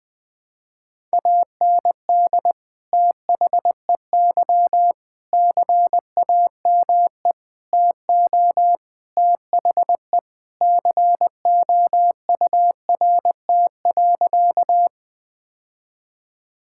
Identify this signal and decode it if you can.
Morse code, heard as a keyed tone, about 20 wpm, 700 Hz. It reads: AND THEY CAME TO THE COURT.
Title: Sentences from Top 1600 Words 20wpm